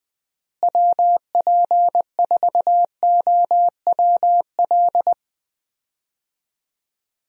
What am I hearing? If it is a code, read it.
Morse code, heard as a keyed tone, about 20 wpm, 700 Hz. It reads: WP4OWL
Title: USA Call Signs - 6 Characters - Repeated 3x - 20wpm